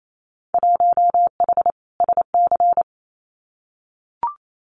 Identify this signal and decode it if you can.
Morse code, heard as a keyed tone, about 28 wpm, 700 Hz. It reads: 1H SC
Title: Winter Field Day Exchanges 28wpm